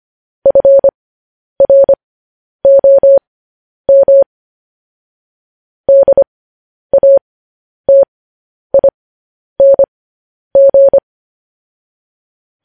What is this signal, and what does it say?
Morse code, heard as a keyed tone, about 25 wpm, spaced as for 10 wpm, 550 Hz. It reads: FROM DATING
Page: news-headlines-in-morse-code-at-10-wpm